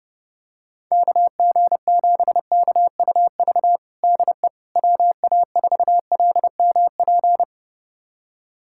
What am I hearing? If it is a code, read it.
Morse code, heard as a keyed tone, about 32 wpm, 700 Hz. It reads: KG7KUV DE WA4LMP
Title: USA Callsign DE Callsign - 6 Characters - 30wpm